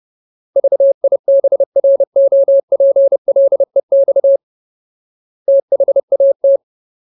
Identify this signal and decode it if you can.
Morse code, heard as a keyed tone, about 30 wpm, 550 Hz. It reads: VIBROPLEX THAT
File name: Ultimate Word List / 30WPM Ultimate